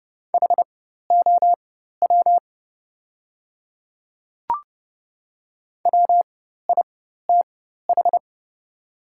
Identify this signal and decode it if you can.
Morse code, heard as a keyed tone, about 29 wpm, spaced as for 14 wpm, 700 Hz. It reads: HOW WITH
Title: 4x Character Spacing - Top 100 Words - 30wpm